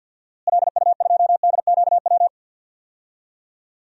Morse code, heard as a keyed tone, about 46 wpm, 700 Hz. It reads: LU2DQW
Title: DXCC-100 Call Signs - 6 Characters - Repeated 3x - 50wpm